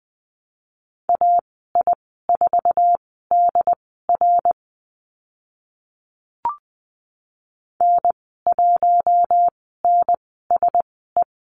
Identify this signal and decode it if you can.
Morse code, heard as a keyed tone, about 20 wpm, spaced as for 15 wpm, 700 Hz. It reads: AI4DR N1NSE
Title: 069 Lesson 12g - Letter d - Call Signs 20wpm